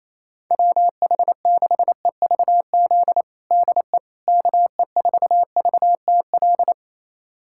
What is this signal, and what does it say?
Morse code, heard as a keyed tone, about 28 wpm, 700 Hz. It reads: WH6EVZ DE KE4VTL
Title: USA Callsign DE Callsign - 6 Characters - 28wpm